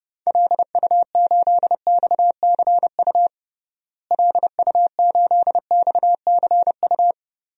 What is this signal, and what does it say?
Morse code, heard as a keyed tone, about 32 wpm, 700 Hz. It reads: LU8XCU LU8XCU
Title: DXCC-100 Call Signs - 6 Characters - Repeated 2x - 30wpm